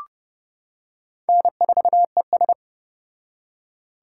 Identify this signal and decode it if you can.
Morse code, heard as a keyed tone, about 29 wpm, 700 Hz. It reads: N4ES